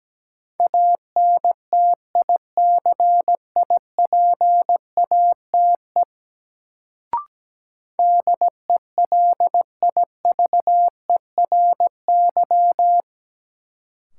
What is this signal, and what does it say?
Morse code, heard as a keyed tone, about 17 wpm, 700 Hz. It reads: ANTICIPATE DELIVERY
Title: Sentences from Top 1500 Words - New Word Drill - 17wpm